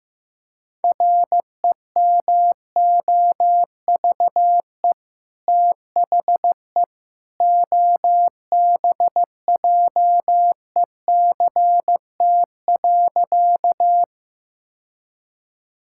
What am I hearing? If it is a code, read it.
Morse code, heard as a keyed tone, about 15 wpm, 700 Hz. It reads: REMOVE THE OBJECT.
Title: Sentences from Top 1000 Words - Review Part 5 - 15wpm